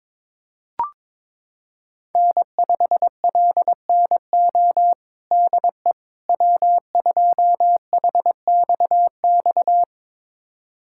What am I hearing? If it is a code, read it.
Morse code, heard as a keyed tone, about 22 wpm, 700 Hz. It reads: N5LNO DE W2HXX
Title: USA Callsign DE Callsign - 5 Characters - 22wpm